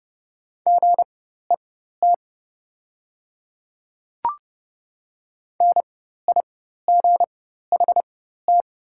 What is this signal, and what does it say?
Morse code, heard as a keyed tone, about 30 wpm, spaced as for 14 wpm, 700 Hz. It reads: GET NIGHT
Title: 4x Character Spacing - Top 300 Words - 30wpm